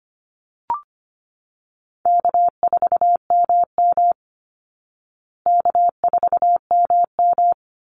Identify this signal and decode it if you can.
Morse code, heard as a keyed tone, about 25 wpm, 700 Hz. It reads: K4MM K4MM
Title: Easy Call Signs Repeated 2x - 25wpm